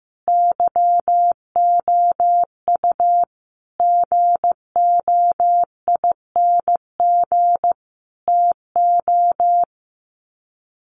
Morse code, heard as a keyed tone, about 15 wpm, 700 Hz. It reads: YOU GOING TO
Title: Sets of 5 Words Encore2 15wpm